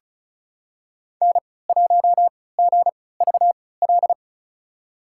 Morse code, heard as a keyed tone, about 36 wpm, spaced as for 20 wpm, 700 Hz. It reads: N1GVL
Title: USA Call Signs - 3x Character Spacing - 35wpm